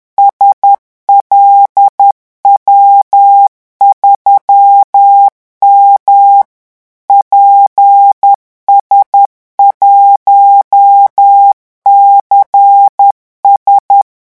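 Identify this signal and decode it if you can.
Morse code, heard as a keyed tone, about 10 wpm, 800 Hz. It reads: SLW3M PS1CS
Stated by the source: Morse revision exercise for French radio amateurs at 10 words per minute.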